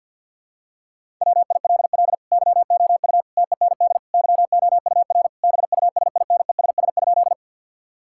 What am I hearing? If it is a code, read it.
Morse code, heard as a keyed tone, about 50 wpm, 700 Hz. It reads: WILL YOU TEND YOUR BUSINESS?
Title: Sentences from Top 1000 Words - Review Part 5 - 50wpm